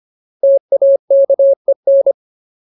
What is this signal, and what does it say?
Morse code, heard as a keyed tone, about 25 wpm, 550 Hz. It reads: TAKEN
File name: Ultimate Word List / 25WPM Ultimate